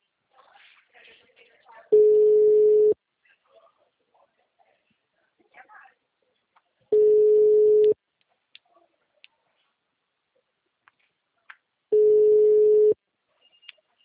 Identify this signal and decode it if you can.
Ringback tone (Europe, 425 Hz), with speech.
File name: out- -2013- - - .3882